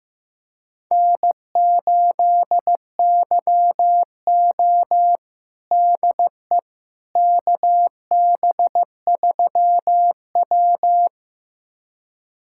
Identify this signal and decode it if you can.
Morse code, heard as a keyed tone, about 15 wpm, 700 Hz. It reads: N8YO DE KB3W
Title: USA Callsign DE Callsign - 4 Characters - 15wpm